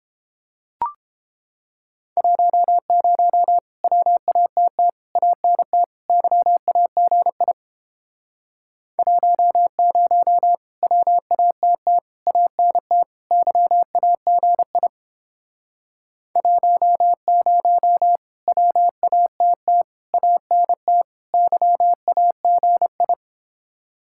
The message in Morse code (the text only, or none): 10 WATT ANT YAGI 10 WATT ANT YAGI 10 WATT ANT YAGI